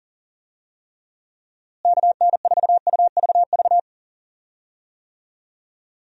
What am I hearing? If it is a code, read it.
Morse code, heard as a keyed tone, about 40 wpm, 700 Hz. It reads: KN4UVV
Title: USA Call Signs - 6 Characters - Repeated 2x - 40wpm